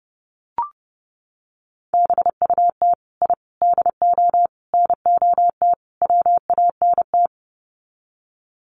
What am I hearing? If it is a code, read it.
Morse code, heard as a keyed tone, about 30 wpm, 700 Hz. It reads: BUT I DO NOT WANT
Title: Sets of 5 Words 30wpm